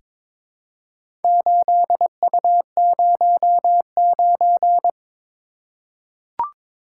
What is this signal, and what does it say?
Morse code, heard as a keyed tone, about 22 wpm, 700 Hz. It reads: 8U09
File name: Four Letter-Number Combinations - 22wpm